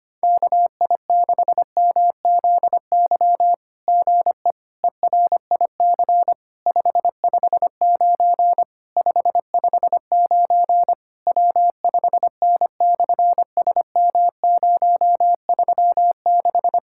Morse code, heard as a keyed tone, about 25 wpm, 700 Hz. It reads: KI6MZY GE ERIC 559 559 W5N/SM036
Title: SOTA QSO Practice 25wpm